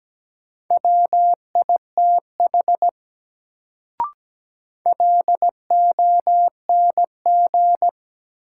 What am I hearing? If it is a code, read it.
Morse code, heard as a keyed tone, about 17 wpm, 700 Hz. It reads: WITH LONG
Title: Sentences from Top 100 Words - New Word Drill - 17wpm